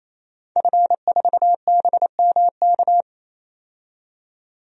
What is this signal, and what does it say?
Morse code, heard as a keyed tone, about 27 wpm, 700 Hz. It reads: F4BMK
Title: DXCC-100 Call Signs - 5 Characters - 28wpm